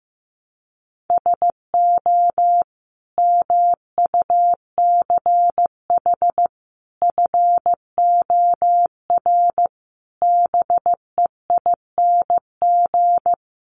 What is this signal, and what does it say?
Morse code, heard as a keyed tone, about 15 wpm, 700 Hz. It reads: SO MUCH FOR BEING
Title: Sets of 5 Words Encore 15wpm